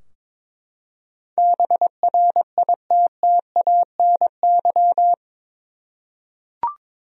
Morse code, heard as a keyed tone, about 22 wpm, 700 Hz. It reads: BRITTANY
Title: USA Common Names - 22wpm